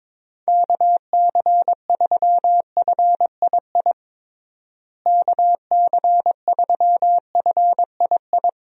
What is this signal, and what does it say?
Morse code, heard as a keyed tone, about 21 wpm, 700 Hz. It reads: KC3FII KC3FII
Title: USA Call Signs - 6 Characters - Repeated 2x - 22wpm